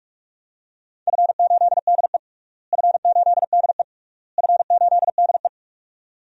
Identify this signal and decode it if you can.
Morse code, heard as a keyed tone, about 44 wpm, 700 Hz. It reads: F8DE F8DE F8DE
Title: DXCC-100 Call Signs - 4 Characters - Repeated 3x - 45wpm